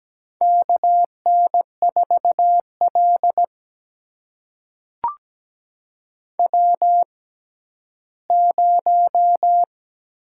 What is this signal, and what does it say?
Morse code, heard as a keyed tone, about 17 wpm, spaced as for 11 wpm, 700 Hz. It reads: KN4L W0
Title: USA Call Signs - 4 Characters - 6x Character Spacing - 17wpm